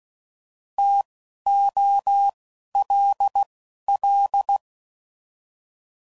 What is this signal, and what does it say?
Morse code, heard as a keyed tone, about 16 wpm, spaced as for 11 wpm, 800 Hz. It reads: TOLL